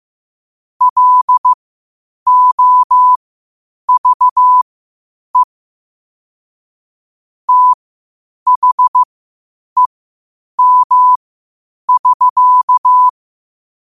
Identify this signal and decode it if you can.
Morse code, heard as a keyed tone, about 15 wpm, spaced as for 8 wpm, 1000 Hz. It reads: LOVE THEM<SK>
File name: qotdfiles / 8wpm